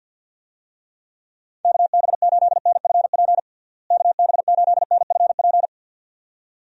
Morse code, heard as a keyed tone, about 46 wpm, 700 Hz. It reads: KB8NFP KB8NFP
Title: USA Call Signs - 6 Characters - Repeated 2x - 50wpm